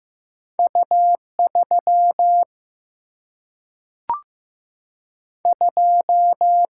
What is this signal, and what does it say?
Morse code, heard as a keyed tone, about 15 wpm, 700 Hz. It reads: U 3 2
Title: Single-Letter Single-Number 15wpm